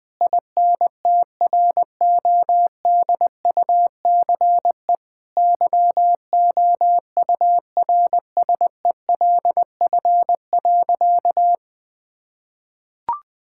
Morse code, heard as a keyed tone, about 20 wpm, 700 Hz. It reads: INTRODUCE YOURSELF.
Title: Sentences from Top 1600 Words 20wpm